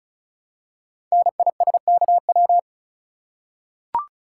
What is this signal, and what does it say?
Morse code, heard as a keyed tone, about 36 wpm, 690 Hz. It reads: NISKW
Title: Code Groups - 5 Letters - 35wpm